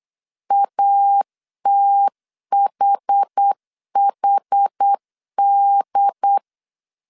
Morse code, heard as a keyed tone, about 8 wpm, 800 Hz. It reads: ATHHD